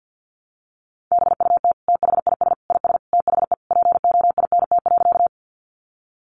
Morse code, heard as a keyed tone, about 50 wpm, 700 Hz. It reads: BUT THIS IS THE POINT.
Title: Sentences from Top 200 Words 50wpm